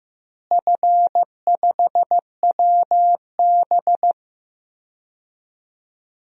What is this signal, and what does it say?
Morse code, heard as a keyed tone, about 15 wpm, 700 Hz. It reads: F5WB
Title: DXCC-100 Call Signs - 4 Characters - Repeated 2x - 15wpm